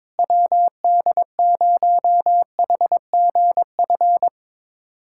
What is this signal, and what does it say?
Morse code, heard as a keyed tone, about 22 wpm, 700 Hz. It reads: WD0HGF
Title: USA Callsign DE Callsign - 6 Characters - 22wpm